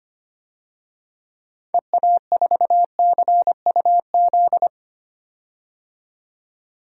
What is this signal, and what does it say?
Morse code, heard as a keyed tone, about 25 wpm, 700 Hz. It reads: EA4CUZ
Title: DXCC-100 Call Signs - 6 Characters - 25wpm